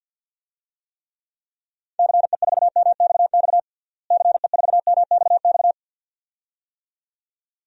Morse code, heard as a keyed tone, about 50 wpm, 700 Hz. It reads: CE4MXX CE4MXX
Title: DXCC-100 Call Signs - 6 Characters - Repeated 2x - 50wpm